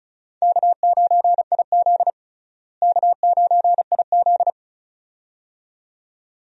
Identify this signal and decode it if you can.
Morse code, heard as a keyed tone, about 36 wpm, 700 Hz. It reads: K9IZ K9IZ
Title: USA Call Signs - 4 Characters - Repeated 2x - 35wpm